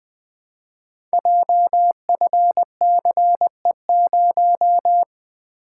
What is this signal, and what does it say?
Morse code, heard as a keyed tone, about 21 wpm, 700 Hz. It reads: JFCE0
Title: Code Groups - 5 Letter:Numbers - 20wpm